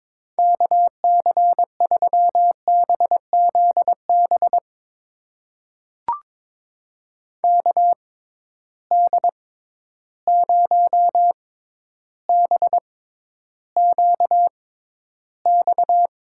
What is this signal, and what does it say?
Morse code, heard as a keyed tone, about 22 wpm, spaced as for 11 wpm, 700 Hz. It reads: KC3BZB KD0BQX
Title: USA Call Signs - 6 Characters - 6x Character Spacing - 22wpm